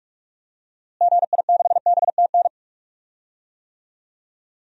Morse code, heard as a keyed tone, about 42 wpm, 700 Hz. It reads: GI6BTN
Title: DXCC-100 Call Signs - 6 Characters - Repeated 3x - 45wpm